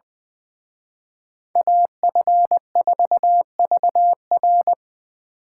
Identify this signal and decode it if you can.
Morse code, heard as a keyed tone, about 24 wpm, 700 Hz. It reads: AF4VR
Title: Speed Racing - USA 2x2 Call Signs 20wpm